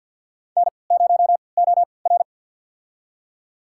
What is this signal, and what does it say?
Morse code, heard as a keyed tone, about 52 wpm, spaced as for 29 wpm, 700 Hz. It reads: N0OR
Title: USA Call Signs - 4 Characters - 3x Character Spacing - 50wpm